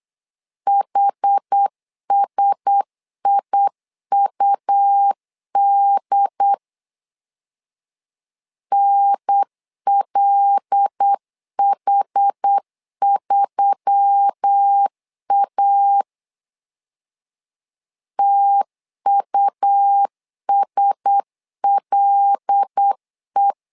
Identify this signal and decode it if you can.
Morse code, heard as a keyed tone, about 8 wpm, 800 Hz. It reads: HSIUD NLH3A TUSLE
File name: LEZIONI / 5 LBVC / 5 L 8wpm 2831